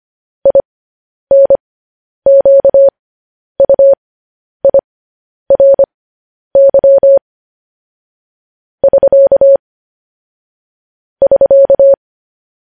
Morse code, heard as a keyed tone, about 25 wpm, spaced as for 10 wpm, 550 Hz. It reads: INQUIRY <SK> <SK>